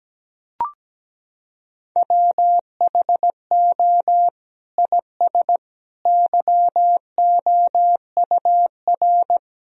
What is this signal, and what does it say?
Morse code, heard as a keyed tone, about 17 wpm, 700 Hz. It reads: WHO IS YOUR
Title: Sentences from Top 1000 Words - Review Part 4 - 17wpm